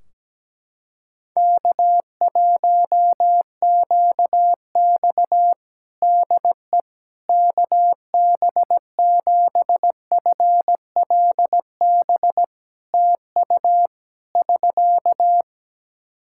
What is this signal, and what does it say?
Morse code, heard as a keyed tone, about 17 wpm, 700 Hz. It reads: K1QX DE KB7FLB TU <SK>
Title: Practice QSO - Exchange 4 - Ending - 17wpm